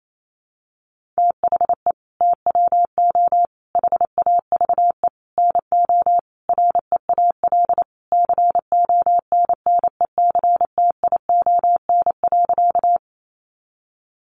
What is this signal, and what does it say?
Morse code, heard as a keyed tone, about 28 wpm, 700 Hz. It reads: THE TWO HAVE NO REAL CONNECTION.
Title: Sentences from Top 1300 Words 28wpm